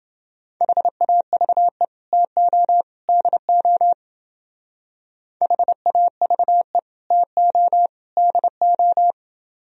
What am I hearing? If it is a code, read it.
Morse code, heard as a keyed tone, about 29 wpm, 700 Hz. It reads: HAVE TO DO HAVE TO DO